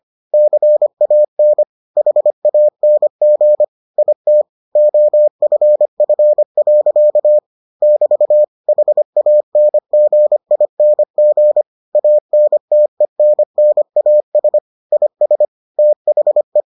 Morse code, heard as a keyed tone, about 25 wpm, 600 Hz. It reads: CAN HANG IT OFF. <BT> HANGING ANTENNAS IS THE